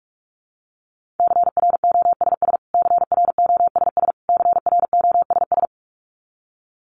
Morse code, heard as a keyed tone, about 44 wpm, 700 Hz. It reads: CROSS CROSS CROSS
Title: Top 500 Words Repeated 45wpm